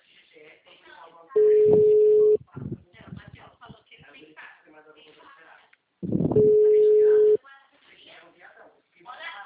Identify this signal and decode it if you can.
Ringback tone (Europe, 425 Hz), with speech.